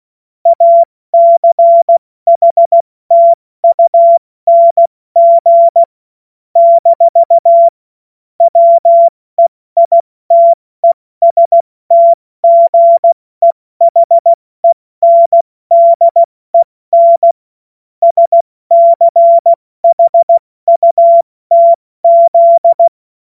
Morse code, heard as a keyed tone, about 16 wpm, spaced as for 14 wpm, 680 Hz. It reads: ACHTUNG - WEITESTGEHENDEN SCHUTZ
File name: Morsekurs / Witze / Witz 04. Mai 2021, 70 F80